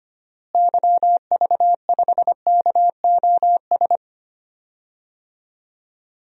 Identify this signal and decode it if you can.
Morse code, heard as a keyed tone, about 25 wpm, 700 Hz. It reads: YV5KOS
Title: DXCC-100 Call Signs - 6 Characters - Repeated 2x - 25wpm